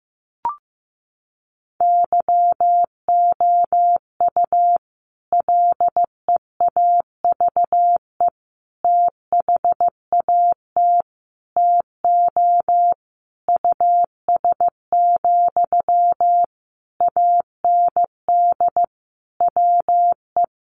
Morse code, heard as a keyed tone, about 15 wpm, 700 Hz. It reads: YOU LEAVE THAT TO US, AND WE
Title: Sentences from Top 1400 Words 15wpm